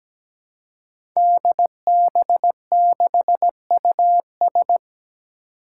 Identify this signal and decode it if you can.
Morse code, heard as a keyed tone, about 17 wpm, 700 Hz. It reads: DB6US
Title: Five Challenging Letter-Number Combinations 17wpm